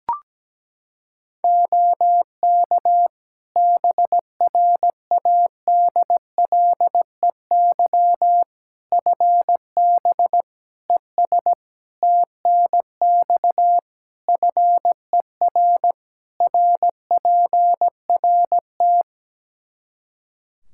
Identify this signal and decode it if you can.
Morse code, heard as a keyed tone, about 17 wpm, 700 Hz. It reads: OK BRADLEY FB ES TNX FER RPRT
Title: Practice QSO - Exchange 2 - Info - 17wpm